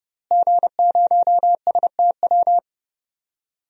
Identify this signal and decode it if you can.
Morse code, heard as a keyed tone, about 30 wpm, 700 Hz. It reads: G0STW
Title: DXCC-100 Call Signs - 5 Characters - Repeated 3x - 30wpm